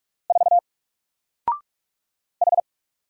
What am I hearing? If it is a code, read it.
Morse code, heard as a keyed tone, about 48 wpm, 700 Hz. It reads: 4 H
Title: Single Character (Letter:Number:Punctuation:Prosign) Alt 45wpm